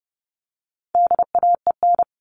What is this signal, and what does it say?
Morse code, heard as a keyed tone, about 30 wpm, 700 Hz. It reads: DAEN
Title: All Cut Numbers - 4-Digit 30wpm